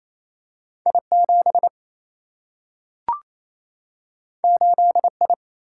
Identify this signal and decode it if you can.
Morse code, heard as a keyed tone, about 28 wpm, 700 Hz. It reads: I 7 8 I
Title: Single-Letter Single-Number 28wpm